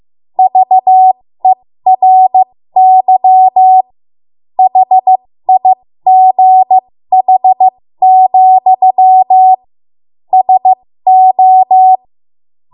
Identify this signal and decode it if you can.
Morse code, heard as a keyed tone, about 15 wpm, spaced as for 13 wpm, 750 Hz. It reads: VERY HIGH, SO
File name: Morse / 13WPM